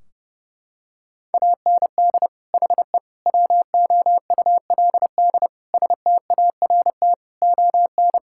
Morse code, heard as a keyed tone, about 32 wpm, 700 Hz. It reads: AND HE WOULD START ON
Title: Sentences from Top 1600 Words 30wpm